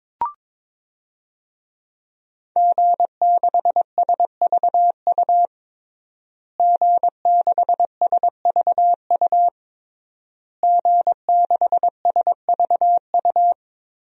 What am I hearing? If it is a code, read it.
Morse code, heard as a keyed tone, about 22 wpm, 700 Hz. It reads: G6SVU G6SVU G6SVU
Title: DXCC-100 Call Signs - 5 Characters - Repeated 3x - 22wpm